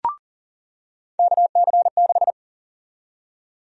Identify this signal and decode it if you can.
Morse code, heard as a keyed tone, about 40 wpm, 700 Hz. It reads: KC6
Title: USA Call Sign Prefixes - 40wpm